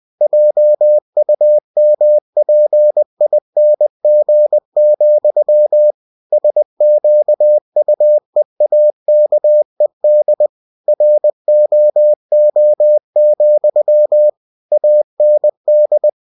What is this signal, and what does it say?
Morse code, heard as a keyed tone, about 20 wpm, 600 Hz. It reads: JUMPING, SQUEAKED ROO, AND
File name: cw-books / winnie-the-pooh / winnie-20wpm-page-0066